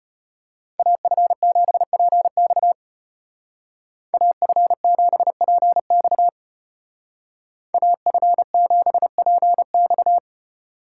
Morse code, heard as a keyed tone, about 30 wpm, 700 Hz. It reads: AF7PX AF7PX AF7PX
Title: Speed Racing - USA 2x2 Call Signs 25wpm